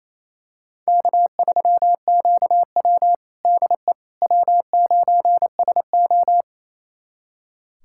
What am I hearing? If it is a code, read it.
Morse code, heard as a keyed tone, about 28 wpm, 700 Hz. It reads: K3QW DE W9SO
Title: USA Callsign DE Callsign - 4 Characters - 28wpm